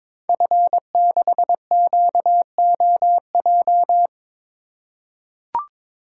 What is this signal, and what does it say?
Morse code, heard as a keyed tone, about 23 wpm, 700 Hz. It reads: F6QOJ
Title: Code Groups - 5 Letter:Numbers - 22wpm